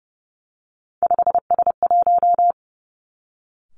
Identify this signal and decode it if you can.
Morse code, heard as a keyed tone, about 30 wpm, 700 Hz. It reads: 5S1
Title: Three Challenging Letter-Number Combinations 30wpm